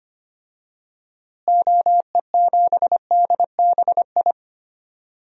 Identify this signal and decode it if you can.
Morse code, heard as a keyed tone, about 25 wpm, 700 Hz. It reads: OE7DBI
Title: DXCC-100 Call Signs - 6 Characters - Repeated 3x - 25wpm